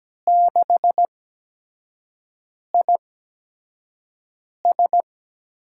Morse code, heard as a keyed tone, about 17 wpm, spaced as for 5 wpm, 700 Hz. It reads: 6IS
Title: USA Call Signs - 6 Characters - 8x Character Spacing - 17wpm